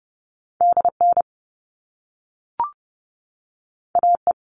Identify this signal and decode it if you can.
Morse code, heard as a keyed tone, about 30 wpm, 700 Hz. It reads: DN AE
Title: All Cut Numbers - 2-Digit 30wpm